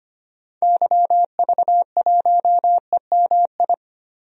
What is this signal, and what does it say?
Morse code, heard as a keyed tone, about 25 wpm, 700 Hz. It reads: YV1EMI
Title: DXCC-100 Call Signs - 6 Characters - 25wpm